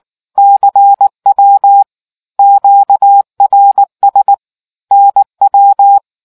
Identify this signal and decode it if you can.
Morse code, heard as a keyed tone, about 19 wpm, 800 Hz. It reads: CW QRS NW